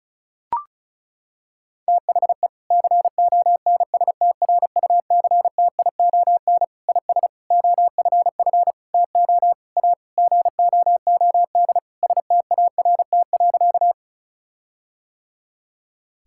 Morse code, heard as a keyed tone, about 36 wpm, 700 Hz. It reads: THE CONSTRUCTION IS OFF TO A GOOD START.
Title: Sentences from Top 1600 Words 35wpm